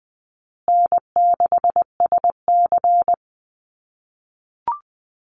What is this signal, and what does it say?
Morse code, heard as a keyed tone, about 20 wpm, 700 Hz. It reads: N6SC
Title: USA Call Signs Repeated 3x 20wpm